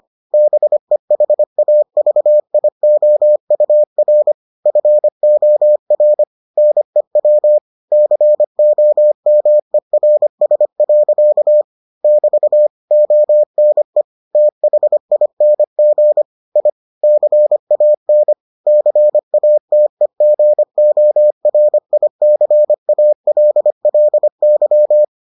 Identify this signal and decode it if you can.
Morse code, heard as a keyed tone, about 25 wpm, 600 Hz. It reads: BEHAVIOUR FOR NEW COMERS. <BT> ONE THING I CAN CATEGORICALLY